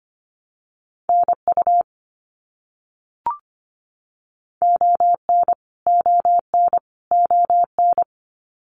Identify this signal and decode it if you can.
Morse code, heard as a keyed tone, about 25 wpm, 700 Hz. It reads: NU ON ON ON
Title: Canadian Provinces and Territories - Repeated 3x - 25wpm